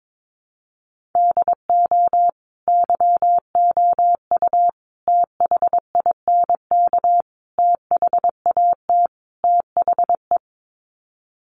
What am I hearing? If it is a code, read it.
Morse code, heard as a keyed tone, about 22 wpm, 700 Hz. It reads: DO YOU THINK THAT THE
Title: Sets of 5 Words Encore 22wpm